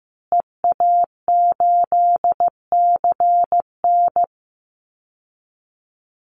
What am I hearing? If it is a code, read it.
Morse code, heard as a keyed tone, about 15 wpm, 700 Hz. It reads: EA8CN
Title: Call Signs Repeated 2x - 15wpm